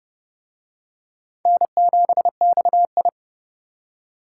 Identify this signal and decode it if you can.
Morse code, heard as a keyed tone, about 29 wpm, 700 Hz. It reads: N7XI